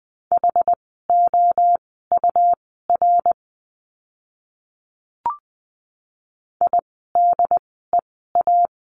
Morse code, heard as a keyed tone, about 20 wpm, spaced as for 14 wpm, 700 Hz. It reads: HOUR IDEA
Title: 108 Lesson 18f - Letter m - Words 20wpm